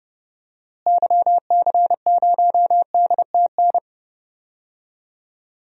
Morse code, heard as a keyed tone, about 30 wpm, 700 Hz. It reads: YC0DTN
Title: DXCC-100 Call Signs - 6 Characters - Repeated 2x - 30wpm